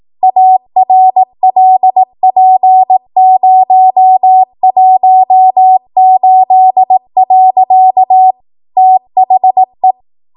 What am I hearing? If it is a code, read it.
Morse code, heard as a keyed tone, about 18 wpm, 750 Hz. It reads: ARLP018. THE
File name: Morse / 18WPM